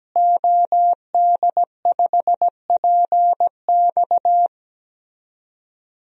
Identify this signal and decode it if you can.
Morse code, heard as a keyed tone, about 17 wpm, 700 Hz. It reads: OD5PX
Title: DXCC-100 Call Signs - 5 Characters - Repeated 3x - 17wpm